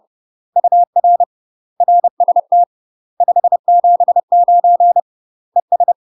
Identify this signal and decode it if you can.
Morse code, heard as a keyed tone, about 30 wpm, 700 Hz. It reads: UR RST 579 ES
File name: UK MASTER QSOs TRAINING / 30WPM / UK MASTER QSOs 0026